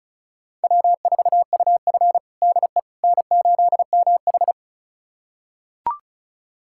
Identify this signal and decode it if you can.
Morse code, heard as a keyed tone, about 36 wpm, 700 Hz. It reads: W4UF DE N8MH
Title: USA Callsign DE Callsign - 4 Characters - 35wpm